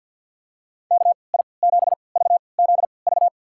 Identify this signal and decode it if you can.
Morse code, heard as a keyed tone, about 52 wpm, spaced as for 29 wpm, 700 Hz. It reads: KI7VZV
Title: USA Call Signs - 6 Characters - 3x Character Spacing - 50wpm